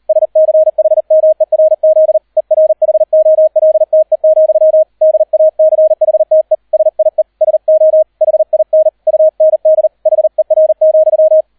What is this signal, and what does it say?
Morse code, heard as a keyed tone, about 38 wpm, 630 Hz. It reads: SCHMERZ ERHOLTE, DACHTE SIE SO HIN UND HER,